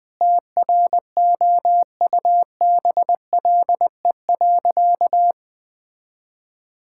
Morse code, heard as a keyed tone, about 20 wpm, 700 Hz. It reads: TROUBLE.
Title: Sentences with 9-Word Phrases - Spoken Beginning - 20wpm